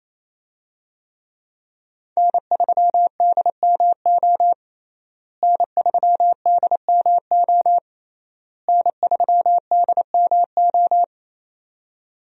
Morse code, heard as a keyed tone, about 27 wpm, 700 Hz. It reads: N3DMO N3DMO N3DMO
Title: USA Call Signs - 5 Characters - Repeated 3x - 28wpm